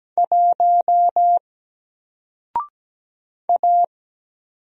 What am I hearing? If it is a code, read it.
Morse code, heard as a keyed tone, about 17 wpm, 700 Hz. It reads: 1 A
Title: Single Letter-Number Alt 17wpm